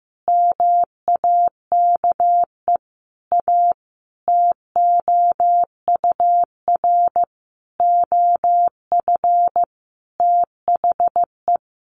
Morse code, heard as a keyed tone, about 15 wpm, 700 Hz. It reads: MAKE A TOUR OF THE
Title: Sentences from Top 1400 Words 15wpm